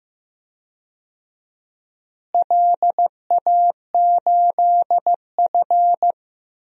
Morse code, heard as a keyed tone, about 15 wpm, 700 Hz. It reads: LA8F
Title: DXCC-100 Call Signs - 4 Characters - Repeated 2x - 15wpm